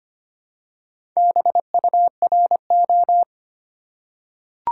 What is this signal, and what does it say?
Morse code, heard as a keyed tone, about 25 wpm, 700 Hz. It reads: BURO
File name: Top 90 QSO Elements - 4x Character Spacing - 25wpm